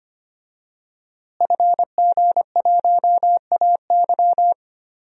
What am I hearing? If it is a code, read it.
Morse code, heard as a keyed tone, about 25 wpm, 700 Hz. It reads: FG1AY
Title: DXCC-100 Call Signs - 5 Characters - Repeated 3x - 25wpm